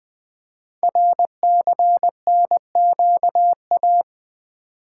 Morse code, heard as a keyed tone, about 20 wpm, 700 Hz. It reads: RCNQA
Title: Code Groups - 5 Letters - 20wpm